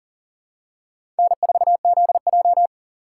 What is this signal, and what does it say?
Morse code, heard as a keyed tone, about 40 wpm, 700 Hz. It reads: N4ZJ
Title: USA Call Signs - 2x Character Spacing - 40wpm